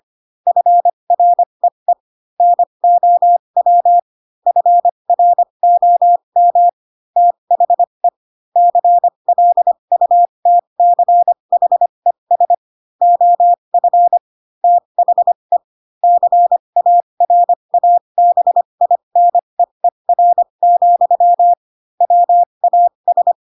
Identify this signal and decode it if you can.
Morse code, heard as a keyed tone, about 25 wpm, spaced as for 22 wpm, 700 Hz. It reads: FREE NOW FROM THE CLUTCHES OF THE CARABINEER, WAS
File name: Pinocchio / Pinocchio ch4 18 25 ik1hge